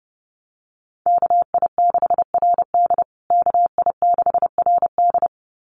Morse code, heard as a keyed tone, about 30 wpm, 700 Hz. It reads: KI6RD KI6RD
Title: USA Call Signs Repeated 2x 30wpm